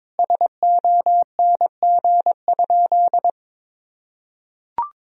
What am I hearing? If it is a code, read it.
Morse code, heard as a keyed tone, about 21 wpm, 700 Hz. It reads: SONG?
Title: Sentences from Top 1000 Words - Review Part 2 - 22wpm